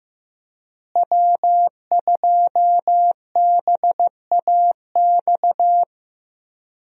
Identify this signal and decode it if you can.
Morse code, heard as a keyed tone, about 15 wpm, 700 Hz. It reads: W2BAX
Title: USA Call Signs - 5 Characters - Repeated 2x - 15wpm